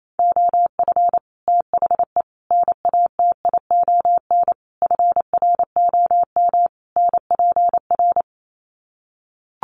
Morse code, heard as a keyed tone, about 28 wpm, 700 Hz. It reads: OF THE NATION FROM NPR
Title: Sets of 5 Words 28wpm